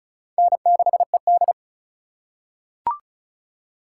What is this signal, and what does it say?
Morse code, heard as a keyed tone, about 36 wpm, 700 Hz. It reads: N6ED
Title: USA Call Signs - 4 Characters - 2x Character Spacing - 35wpm